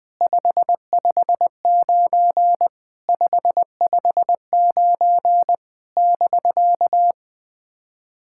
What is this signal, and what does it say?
Morse code, heard as a keyed tone, about 20 wpm, 690 Hz. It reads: 559 559 <BK>
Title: SOTA QSO Practice 20wpm